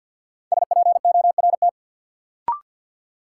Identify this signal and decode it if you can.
Morse code, heard as a keyed tone, about 53 wpm, 700 Hz. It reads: SPORT
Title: Sentences from Top 1000 Words - Review Part 4 - New Word Drill - 50wpm